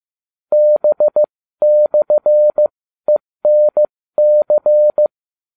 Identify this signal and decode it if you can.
Morse code, heard as a keyed tone, about 15 wpm, spaced as for 13 wpm, 600 Hz. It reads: B/ENC